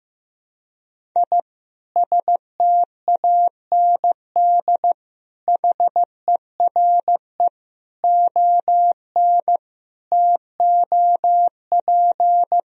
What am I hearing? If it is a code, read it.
Morse code, heard as a keyed tone, about 15 wpm, 700 Hz. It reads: I STAND HERE ON TOP
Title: Sentences from Top 1000 Words - Review Part 1 - 15wpm